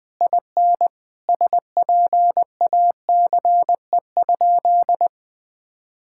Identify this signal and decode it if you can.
Morse code, heard as a keyed tone, about 20 wpm, 700 Hz. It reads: IN SPACE?
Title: Sentences from Top 1000 Words - Review Part 5 - 20wpm